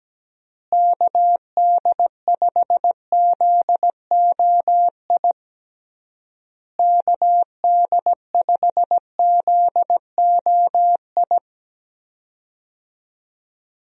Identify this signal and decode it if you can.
Morse code, heard as a keyed tone, about 17 wpm, 700 Hz. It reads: KD5ZOI KD5ZOI
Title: USA Call Signs - 6 Characters - Repeated 3x - 17wpm